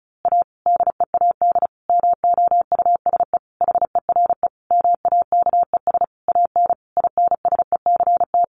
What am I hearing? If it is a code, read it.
Morse code, heard as a keyed tone, about 34 wpm, 700 Hz. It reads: A DEAD MOUSE HERE MAKES AN INSECT
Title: A Sound Of Thunder 35wpm